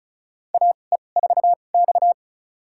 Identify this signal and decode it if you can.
Morse code, heard as a keyed tone, about 36 wpm, spaced as for 25 wpm, 700 Hz. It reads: AE4X
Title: USA Call Signs - 4 Characters - 2x Character Spacing - 35wpm